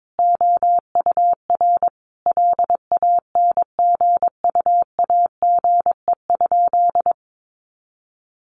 Morse code, heard as a keyed tone, about 22 wpm, 700 Hz. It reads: OUR LANGUAGE?
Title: Sentences from Top 900 Words 22wpm